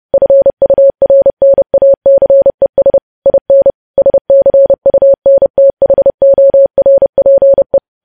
Morse code, heard as a keyed tone, about 30 wpm, 550 Hz. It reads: FURNACES IN SCUNTHORPE
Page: news headlines in morse code at 30 wpm